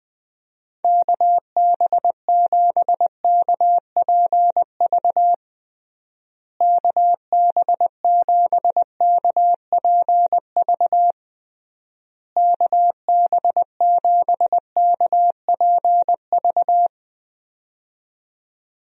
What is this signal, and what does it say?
Morse code, heard as a keyed tone, about 20 wpm, 700 Hz. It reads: KB7KPV KB7KPV KB7KPV
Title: USA Call Signs - 6 Characters - Repeated 3x - 20wpm